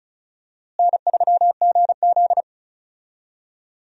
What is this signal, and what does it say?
Morse code, heard as a keyed tone, about 36 wpm, 700 Hz. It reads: N3GZ
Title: USA Call Signs - 4 Characters - Repeated 3x - 35wpm